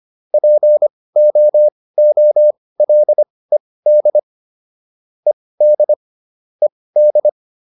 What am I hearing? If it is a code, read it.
Morse code, heard as a keyed tone, about 25 wpm, spaced as for 18 wpm, 600 Hz. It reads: POOLED ED ED